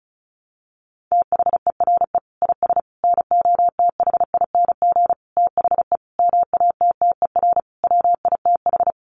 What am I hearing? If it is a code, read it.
Morse code, heard as a keyed tone, about 34 wpm, 700 Hz. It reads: THERE IS NOTHING THE MATTER WITH
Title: Sentences from Top 700 Words 35wpm